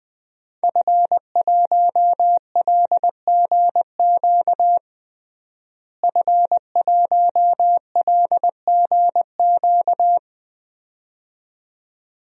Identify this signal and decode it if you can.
Morse code, heard as a keyed tone, about 20 wpm, 700 Hz. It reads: F1LGQ F1LGQ
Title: DXCC-100 Call Signs - 5 Characters - Repeated 2x - 20wpm